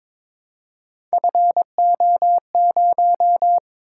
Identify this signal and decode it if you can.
Morse code, heard as a keyed tone, about 23 wpm, 700 Hz. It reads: FO0
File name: practice / 22wpm / Three Letter-Number Combinations - 22wpm